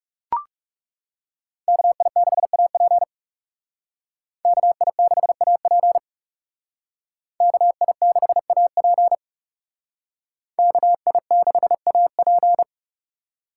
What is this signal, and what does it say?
Morse code, heard as a keyed tone, about 38 wpm, 700 Hz. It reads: KI6AP KI6AP KI6AP KI6AP